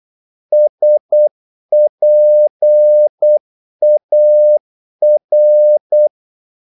Morse code, heard as a keyed tone, about 8 wpm, 600 Hz. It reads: SPAR
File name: Korte ord DK 8wpm